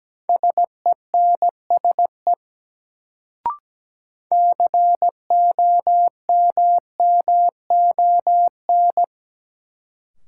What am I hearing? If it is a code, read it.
Morse code, heard as a keyed tone, about 17 wpm, 700 Hz. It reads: SENSE COMMON
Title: Sentences from Top 700 Words - New Word Drill - 17wpm